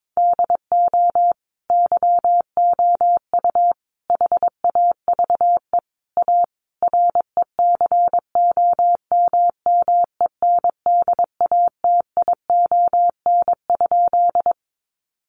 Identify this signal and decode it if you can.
Morse code, heard as a keyed tone, about 22 wpm, 700 Hz. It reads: DO YOU HAVE A RECOMMENDATION?
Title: Sentences from Top 1400 Words 22wpm